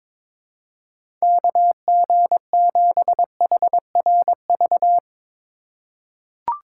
Morse code, heard as a keyed tone, about 22 wpm, 700 Hz. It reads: KG7HRV
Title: USA Call Signs - 6 Characters - 4x Character Spacing - 22wpm